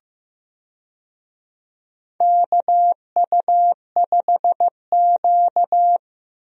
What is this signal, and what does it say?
Morse code, heard as a keyed tone, about 15 wpm, 700 Hz. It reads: KU5Q
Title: USA Call Signs - 4 Characters - Repeated 3x - 15wpm